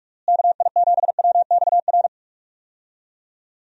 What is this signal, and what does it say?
Morse code, heard as a keyed tone, about 46 wpm, 700 Hz. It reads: KI7WXR
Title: USA Call Signs - 6 Characters - Repeated 3x - 45wpm